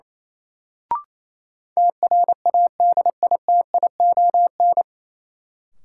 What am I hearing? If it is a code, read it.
Morse code, heard as a keyed tone, about 28 wpm, 700 Hz. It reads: TRADITION
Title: Sentences from Top 1500 Words - New Word Drill - 28wpm